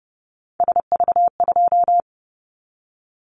Morse code, heard as a keyed tone, about 30 wpm, 700 Hz. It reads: SV2
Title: Callsign Prefixes for Top 50 DXCC - 30wpm